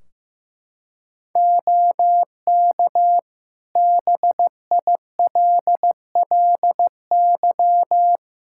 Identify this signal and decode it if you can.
Morse code, heard as a keyed tone, about 15 wpm, 700 Hz. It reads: OK BILLY
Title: Practice QSO - Exchange 4 - Ending - 15wpm